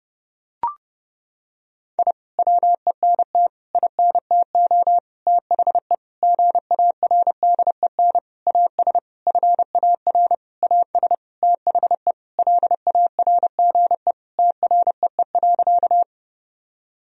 Morse code, heard as a keyed tone, about 30 wpm, 700 Hz. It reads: I WENT INTO THE GARDEN AS FAR AS THE LARGE TREE.
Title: Sentences from Top 1000 Words - Review Part 4 - 30wpm